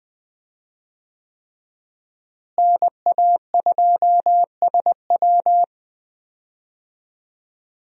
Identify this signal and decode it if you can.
Morse code, heard as a keyed tone, about 20 wpm, 700 Hz. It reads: NA2SW
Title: USA Call Signs - 5 Characters - Repeated 3x - 20wpm